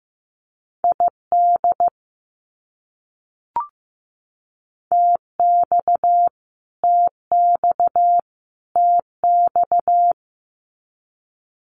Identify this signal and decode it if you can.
Morse code, heard as a keyed tone, about 15 wpm, 700 Hz. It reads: ID TX TX TX
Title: US States Repeated 3x 15wpm